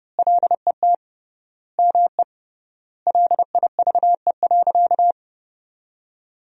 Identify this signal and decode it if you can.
Morse code, heard as a keyed tone, about 30 wpm, 700 Hz. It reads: LET ME LIVE.
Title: 3x Word-Spacing - Sentences from Top 300 Words - 30wpm